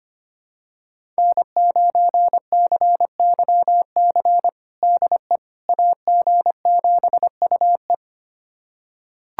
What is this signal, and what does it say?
Morse code, heard as a keyed tone, about 25 wpm, 700 Hz. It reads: N9CYC DE AG7UE
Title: USA Callsign DE Callsign - 5 Characters - 25wpm